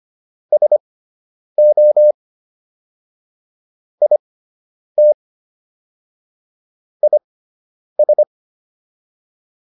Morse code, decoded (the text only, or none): SO IT IS